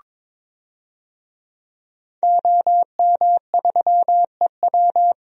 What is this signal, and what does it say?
Morse code, heard as a keyed tone, about 22 wpm, 700 Hz. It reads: OM3EW
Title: DXCC-100 Call Signs - 5 Characters - 22wpm